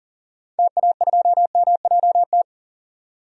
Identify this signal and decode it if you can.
Morse code, heard as a keyed tone, about 40 wpm, 700 Hz. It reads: TA2MJT
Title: DXCC-100 Call Signs - 6 Characters - 40wpm